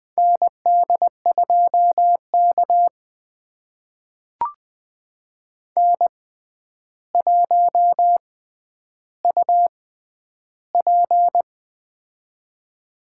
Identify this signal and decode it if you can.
Morse code, heard as a keyed tone, about 20 wpm, spaced as for 10 wpm, 700 Hz. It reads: ND2K N1UP
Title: USA Call Signs - 4 Characters - 6x Character Spacing - 20wpm